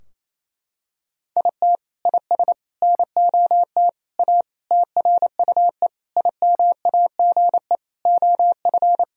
Morse code, heard as a keyed tone, about 28 wpm, 700 Hz. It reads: IT IS NOT A TRUE IMAGE OF
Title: Sentences from Top 1000 Words - Review Part 4 - 28wpm